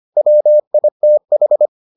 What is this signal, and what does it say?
Morse code, heard as a keyed tone, about 25 wpm, 600 Hz. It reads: WITH